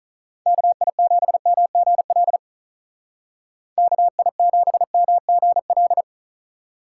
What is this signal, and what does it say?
Morse code, heard as a keyed tone, about 43 wpm, 700 Hz. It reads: KI7MGL KI7MGL
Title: Speed Racing - USA 2x3 Call Signs 35wpm